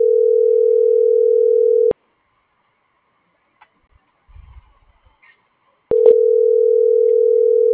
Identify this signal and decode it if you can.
Ringback tone (North America), 440 + 480 Hz.